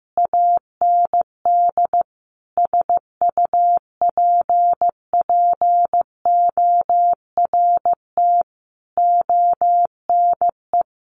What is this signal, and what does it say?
Morse code, heard as a keyed tone, about 15 wpm, 700 Hz. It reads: AND SUPPORT ONE
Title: Sentences from Top 1400 Words 15wpm